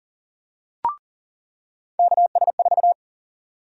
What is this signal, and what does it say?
Morse code, heard as a keyed tone, about 40 wpm, 700 Hz. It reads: KS4
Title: USA Call Sign Prefixes - 40wpm